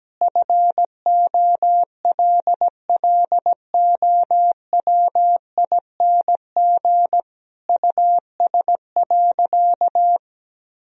Morse code, heard as a keyed tone, about 17 wpm, 700 Hz. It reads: FOLLOWING US.
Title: Sentences from Top 1000 Words - Review Part 4 - 17wpm